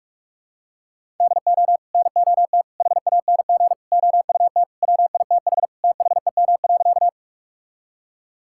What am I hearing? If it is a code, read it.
Morse code, heard as a keyed tone, about 46 wpm, 700 Hz. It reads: DO NOT HANG OUT WITH THEM.
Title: Sentences from Top 1600 Words 45wpm